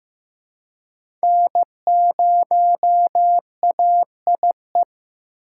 Morse code, heard as a keyed tone, about 15 wpm, 700 Hz. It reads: N0AIE
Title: K1USN SST QSO Practice - 15wpm